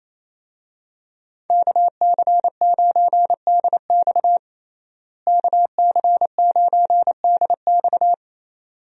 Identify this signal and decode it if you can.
Morse code, heard as a keyed tone, about 28 wpm, 700 Hz. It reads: KC9DX KC9DX
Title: USA Call Signs - 5 Characters - Repeated 2x - 28wpm